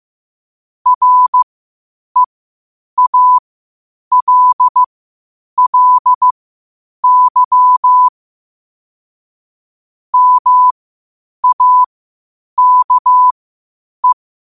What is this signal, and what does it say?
Morse code, heard as a keyed tone, about 15 wpm, spaced as for 8 wpm, 1000 Hz. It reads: REALLY MAKE
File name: qotdfiles / 8wpm